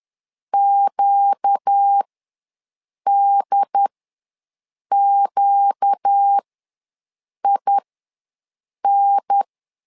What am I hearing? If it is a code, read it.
Morse code, heard as a keyed tone, about 10 wpm, spaced as for 6 wpm, 800 Hz. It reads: QDQIN
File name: LEZIONI / 4 Q ws 10wpm 2306